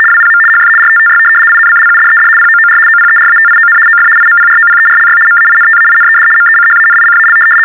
RTTY, about 75 baud. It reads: WPM HF RTTY INCLUDED POOR OPERATION OF IMPROPERLY MAINTAINED MECHANICAL
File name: teletype-RTTY-75